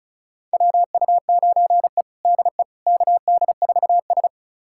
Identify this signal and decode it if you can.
Morse code, heard as a keyed tone, about 36 wpm, 700 Hz. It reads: WU9E DE KD4S
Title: USA Callsign DE Callsign - 4 Characters - 35wpm